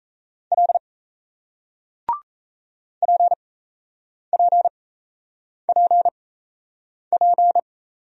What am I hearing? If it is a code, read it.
Morse code, heard as a keyed tone, about 38 wpm, spaced as for 8 wpm, 700 Hz. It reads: L PPPP